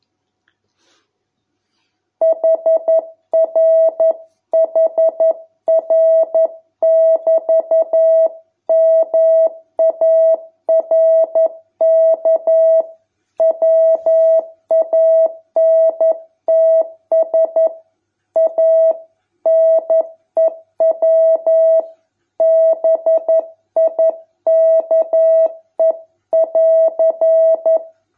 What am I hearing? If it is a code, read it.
Morse code, heard as a keyed tone, about 11 wpm, 650 Hz. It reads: HRHR<BT>MARK WANTS A NEW BIKE<AR>